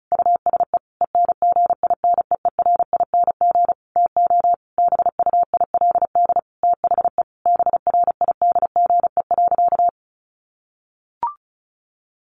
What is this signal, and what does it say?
Morse code, heard as a keyed tone, about 34 wpm, 700 Hz. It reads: USE ENGINEERING TO BUILD THE BRIDGE.
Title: Sentences from Top 1500 Words 35wpm